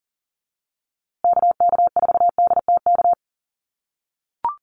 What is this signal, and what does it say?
Morse code, heard as a keyed tone, about 40 wpm, 700 Hz. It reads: KK4DTK
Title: USA Call Signs Repeated 2x 40wpm